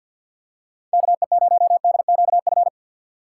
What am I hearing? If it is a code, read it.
Morse code, heard as a keyed tone, about 53 wpm, 700 Hz. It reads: KE0DQF
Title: USA Call Signs - 6 Characters - 4x Character Spacing - 50wpm